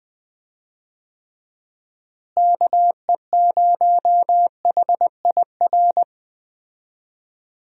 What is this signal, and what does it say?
Morse code, heard as a keyed tone, about 20 wpm, 700 Hz. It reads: KE0HIR
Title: USA Call Signs - 6 Characters - Repeated 3x - 20wpm